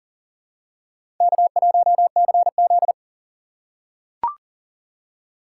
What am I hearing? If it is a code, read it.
Morse code, heard as a keyed tone, about 40 wpm, 700 Hz. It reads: K1CZ
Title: USA Call Signs - 3x Character Spacing - 40wpm